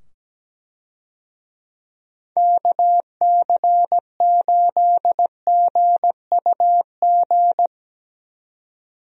Morse code, heard as a keyed tone, about 17 wpm, 700 Hz. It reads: KC8GUG
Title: USA Call Signs - 6 Characters - 17wpm